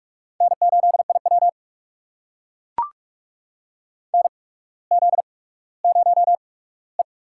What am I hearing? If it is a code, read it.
Morse code, heard as a keyed tone, about 44 wpm, spaced as for 19 wpm, 690 Hz. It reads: N8IW NZ0E
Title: USA Call Signs - 4 Characters - 8x Character Spacing - 45wpm